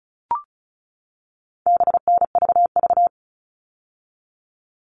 Morse code, heard as a keyed tone, about 34 wpm, 700 Hz. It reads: BNVV
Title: All Cut Numbers - 4-Digit 35wpm